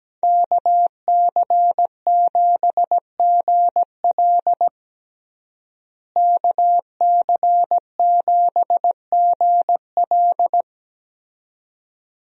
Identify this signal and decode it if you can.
Morse code, heard as a keyed tone, about 17 wpm, 700 Hz. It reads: KC7GL KC7GL
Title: USA Call Signs - 5 Characters - Repeated 2x - 17wpm